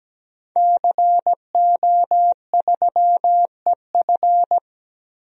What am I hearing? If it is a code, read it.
Morse code, heard as a keyed tone, about 17 wpm, 700 Hz. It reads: CO3EF
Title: DXCC-100 Call Signs - 5 Characters - 17wpm